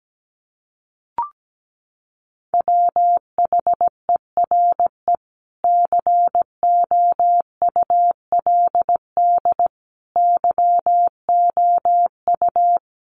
Morse code, heard as a keyed tone, about 17 wpm, 700 Hz. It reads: WHERE COULD YOU
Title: Sentences from Top 300 Words 17wpm